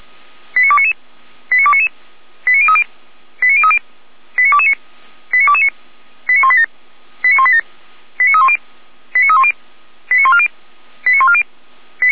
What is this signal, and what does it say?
Five-tone selective call (ZVEI), heard as quick runs of five tones, tones of about 70 ms. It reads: 89299 89299 89030 89030 89229 89229 89187 89187 89310 89310 89140 89140